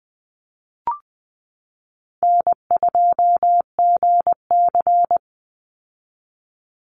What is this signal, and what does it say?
Morse code, heard as a keyed tone, about 20 wpm, 700 Hz. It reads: N2GC
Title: Easy Call Signs - 20wpm